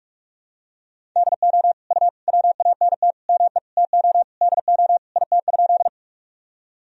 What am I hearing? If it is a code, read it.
Morse code, heard as a keyed tone, about 43 wpm, 700 Hz. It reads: DO U WANT ME TO DO IT?
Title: Sentences from Top 200 Words and Abbreviations 45wpm